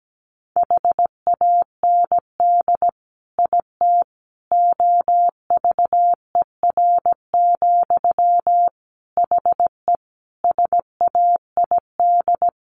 Morse code, heard as a keyed tone, about 17 wpm, 700 Hz. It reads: HAND IT OVER, HE SAID
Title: Sentences from Top 1500 Words 17wpm